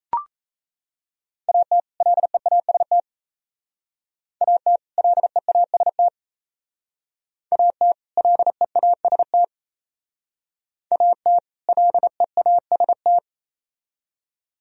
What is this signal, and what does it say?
Morse code, heard as a keyed tone, about 36 wpm, 700 Hz. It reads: AT LEAST AT LEAST AT LEAST AT LEAST